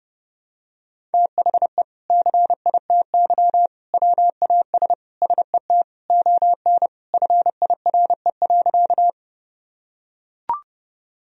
Morse code, heard as a keyed tone, about 32 wpm, 700 Hz. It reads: THE CITY WAS SET ON FIRE.
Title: Sentences from Top 1000 Words - Review Part 1 - 30wpm